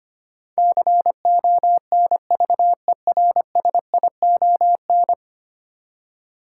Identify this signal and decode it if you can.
Morse code, heard as a keyed tone, about 25 wpm, 700 Hz. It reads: CONVERSION
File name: 10 Letter Words - First 3 Letters Given - 25wpm